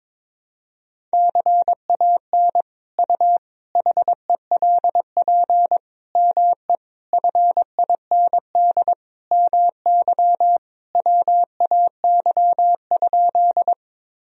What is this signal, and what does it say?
Morse code, heard as a keyed tone, about 21 wpm, 700 Hz. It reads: CAN U HELP ME FIND MY WAY?
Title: Sentences from Top 200 Words and Abbreviations 22wpm